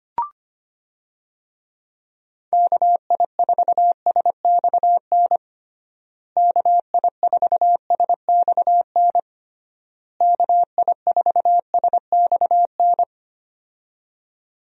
Morse code, heard as a keyed tone, about 25 wpm, 700 Hz. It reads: KI4SXN KI4SXN KI4SXN
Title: USA Call Signs - 6 Characters - Repeated 3x - 25wpm